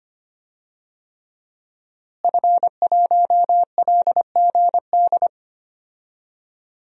Morse code, heard as a keyed tone, about 25 wpm, 700 Hz. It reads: F1LGD
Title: DXCC-100 Call Signs - 5 Characters - Repeated 3x - 25wpm